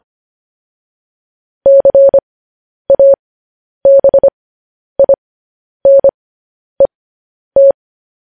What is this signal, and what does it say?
Morse code, heard as a keyed tone, about 25 wpm, spaced as for 10 wpm, 550 Hz. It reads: CABINET
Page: news-headlines-in-morse-code-at-10-wpm